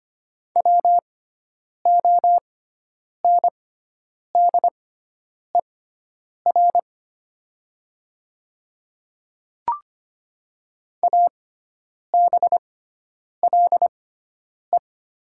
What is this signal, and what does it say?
Morse code, heard as a keyed tone, about 25 wpm, spaced as for 9 wpm, 700 Hz. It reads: WONDER ABLE
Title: 6x Character Spacing - Top 500 Words - 25wpm